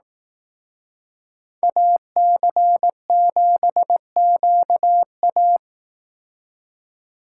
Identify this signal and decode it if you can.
Morse code, heard as a keyed tone, about 18 wpm, 700 Hz. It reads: AC7QA
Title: Speed Racing - USA 2x2 Call Signs 15wpm